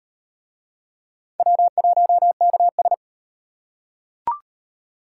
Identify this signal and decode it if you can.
Morse code, heard as a keyed tone, about 38 wpm, 700 Hz. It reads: W1KS